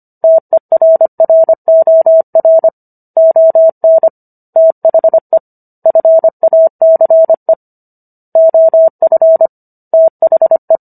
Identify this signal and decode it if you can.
Morse code, heard as a keyed tone, about 25 wpm, 650 Hz. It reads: TERROR ON THE FACE OF THE